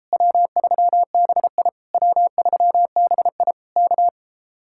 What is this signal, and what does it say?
Morse code, heard as a keyed tone, about 37 wpm, 700 Hz. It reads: W3BI W3BI K